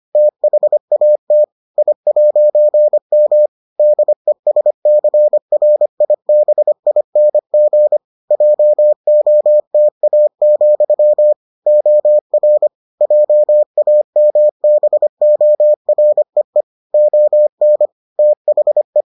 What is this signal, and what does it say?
Morse code, heard as a keyed tone, about 25 wpm, 600 Hz. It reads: THAT I'M DESCRIBING JOTA, OR JAMBOREE ON THE